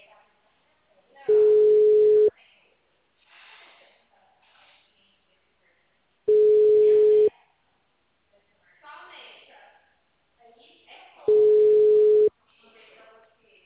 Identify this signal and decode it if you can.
Ringback tone (Europe, 425 Hz), with speech.
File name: out- -2044- - - .61886